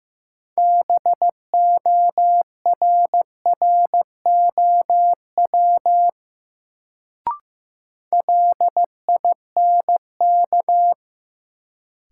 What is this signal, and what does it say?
Morse code, heard as a keyed tone, about 15 wpm, 700 Hz. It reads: BORROW LINK
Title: Sentences from Top 1500 Words - New Word Drill Alt - 15wpm